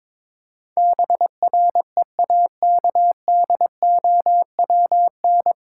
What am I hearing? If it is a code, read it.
Morse code, heard as a keyed tone, about 21 wpm, 700 Hz. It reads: BREAKDOWN
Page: Morse Code Ninja practice 22wpm